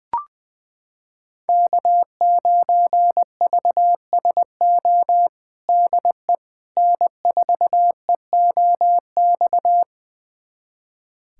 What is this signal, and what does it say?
Morse code, heard as a keyed tone, about 20 wpm, 700 Hz. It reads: K9VSO DE N4EOX
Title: USA Callsign DE Callsign - 5 Characters - 20wpm